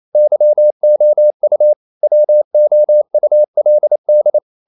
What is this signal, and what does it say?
Morse code, heard as a keyed tone, about 28 wpm, 600 Hz. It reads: YOU WOULD
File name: CW / Practice / EnglishSayings28